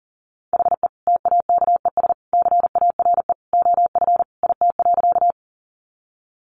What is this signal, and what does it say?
Morse code, heard as a keyed tone, about 37 wpm, 700 Hz. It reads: HE TAKES CARE OF IT.
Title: Additional Easy Phrases 40wpm